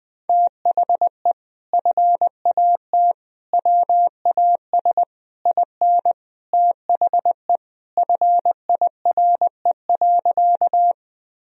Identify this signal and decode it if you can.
Morse code, heard as a keyed tone, about 20 wpm, 700 Hz. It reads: THE FAT WAS IN THE FIRE.
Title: Sentences from Top 1000 Words - Review Part 4 - 20wpm